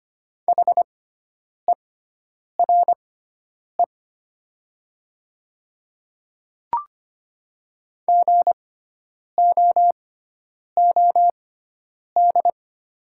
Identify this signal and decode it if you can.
Morse code, heard as a keyed tone, about 25 wpm, spaced as for 9 wpm, 700 Hz. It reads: HERE GOOD
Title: 6x Character Spacing - Top 300 Words - 25wpm